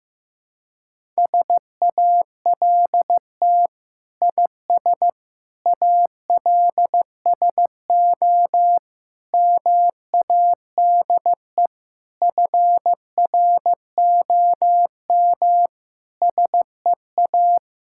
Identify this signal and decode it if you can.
Morse code, heard as a keyed tone, about 16 wpm, 700 Hz. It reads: SALT IS ALSO MADE FROM SEA
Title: Sentences from Top 1000 Words - Review Part 5 - 15wpm